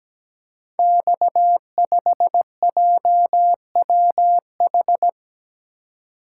Morse code, heard as a keyed tone, about 17 wpm, 700 Hz. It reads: X5JWH
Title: Code Groups - 5 Letter:Numbers - 17wpm